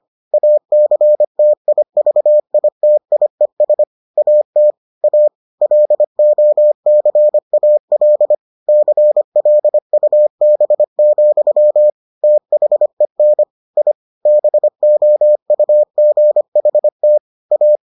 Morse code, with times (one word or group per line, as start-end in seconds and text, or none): 0.3-3.8 ACTIVITIES
4.2-4.7 AT
5.0-5.3 A
5.6-8.3 LOCAL
8.7-11.9 CLUB,
12.2-13.4 THEN
13.8-13.9 I
14.3-17.2 BOUGHT
17.5-17.8 A